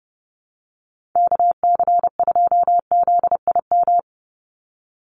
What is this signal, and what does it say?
Morse code, heard as a keyed tone, about 28 wpm, 700 Hz. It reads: KC2ZIM
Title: USA Call Signs 30wpm